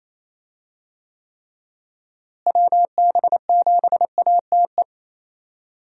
Morse code, heard as a keyed tone, about 28 wpm, 700 Hz. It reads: WB7ATE
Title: USA Call Signs - 6 Characters - Repeated 2x - 28wpm